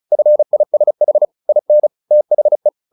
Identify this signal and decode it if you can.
Morse code, heard as a keyed tone, about 34 wpm, 600 Hz. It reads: FISH IN THE
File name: the old man and the sea / the old man and the sea 1 35wpm